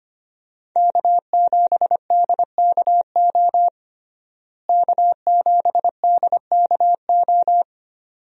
Morse code, heard as a keyed tone, about 25 wpm, 700 Hz. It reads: K7DKO K7DKO
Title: USA Call Signs - 5 Characters - Repeated 2x - 25wpm